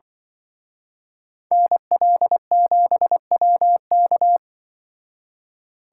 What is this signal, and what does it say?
Morse code, heard as a keyed tone, about 23 wpm, 700 Hz. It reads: NL7WK
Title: Speed Racing - USA 2x2 Call Signs 20wpm